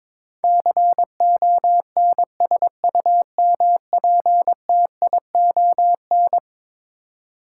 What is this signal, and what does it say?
Morse code, heard as a keyed tone, about 22 wpm, 700 Hz. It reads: CONSUMPTION
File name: practice / 22wpm / Eleven Letter Words - 22wpm